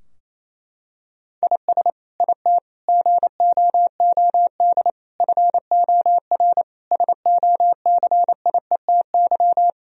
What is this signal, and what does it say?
Morse code, heard as a keyed tone, about 28 wpm, 700 Hz. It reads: IS IT GOOD FOR SOCIETY
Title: Sentences from Top 1000 Words - Review Part 4 - 28wpm